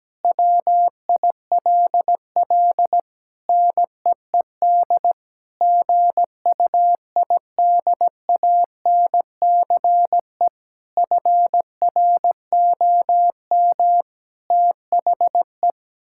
Morse code, heard as a keyed tone, about 17 wpm, 700 Hz. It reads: WILL NEED GUIDANCE FROM THE
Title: Sentences from Top 1600 Words 17wpm